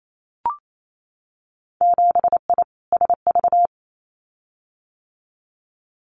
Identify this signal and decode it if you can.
Morse code, heard as a keyed tone, about 28 wpm, 700 Hz. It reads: 7I SV
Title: Winter Field Day Exchanges Encore 28wpm